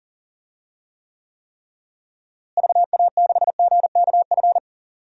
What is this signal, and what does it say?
Morse code, heard as a keyed tone, about 40 wpm, 700 Hz. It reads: VA6GKF
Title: DXCC-100 Call Signs - 6 Characters - Repeated 3x - 40wpm